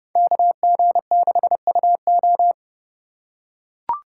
Morse code, heard as a keyed tone, about 30 wpm, 700 Hz. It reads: KG6UO
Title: USA Call Signs - 5 Characters - 2x Character Spacing - 30wpm